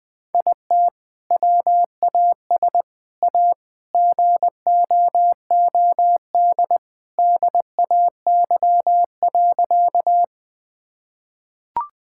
Morse code, heard as a keyed tone, about 19 wpm, 700 Hz. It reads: IT WAS A GOOD DAY.
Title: Sentences with 2-Word Phrases - Spoken Beginning - 20wpm